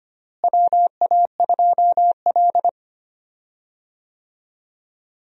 Morse code, heard as a keyed tone, about 25 wpm, 700 Hz. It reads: WA2L
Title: USA Call Signs - 4 Characters - Repeated 2x - 25wpm